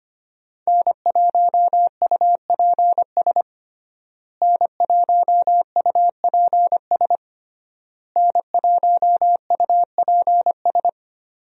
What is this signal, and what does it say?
Morse code, heard as a keyed tone, about 25 wpm, 700 Hz. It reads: N1UPS N1UPS N1UPS
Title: USA Call Signs - 5 Characters - Repeated 3x - 25wpm